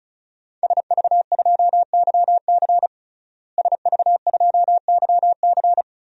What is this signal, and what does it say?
Morse code, heard as a keyed tone, about 36 wpm, 700 Hz. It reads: SV2YC SV2YC
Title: DXCC-100 Call Signs - 5 Characters - Repeated 3x - 35wpm